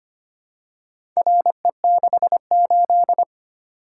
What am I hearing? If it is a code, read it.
Morse code, heard as a keyed tone, about 25 wpm, 700 Hz. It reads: RE68
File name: practice / 25wpm / Four Letter-Number Combinations - 25wpm